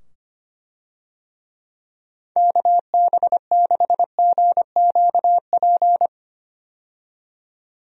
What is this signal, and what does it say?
Morse code, heard as a keyed tone, about 25 wpm, 700 Hz. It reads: KB6GQP
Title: USA Call Signs - 6 Characters - 25wpm